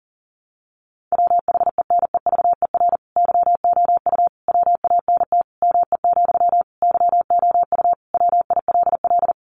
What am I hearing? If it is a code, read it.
Morse code, heard as a keyed tone, about 40 wpm, 700 Hz. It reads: WHENEVER YOU WANT ME, YOU WILL
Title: Sentences from Top 1200 Words 40wpm